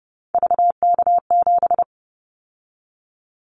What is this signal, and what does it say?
Morse code, heard as a keyed tone, about 30 wpm, 700 Hz. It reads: VK7
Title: Callsign Prefixes for Top 50 DXCC - 30wpm